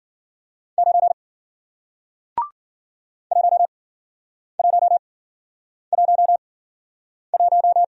Morse code, heard as a keyed tone, about 51 wpm, spaced as for 9 wpm, 700 Hz. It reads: 9 1111